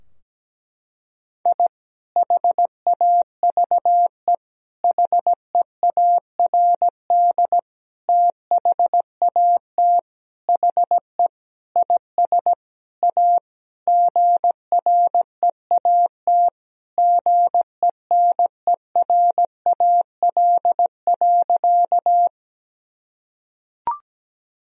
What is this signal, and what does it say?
Morse code, heard as a keyed tone, about 17 wpm, 700 Hz. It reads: I HAVE HEARD THAT HE IS A GREAT GENERAL.
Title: Sentences from Top 1000 Words - Review Part 3 - 17wpm